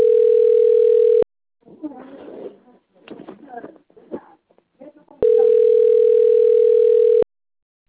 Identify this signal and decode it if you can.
Ringback tone (North America), 440 + 480 Hz, with speech.
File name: out- - - - - .93805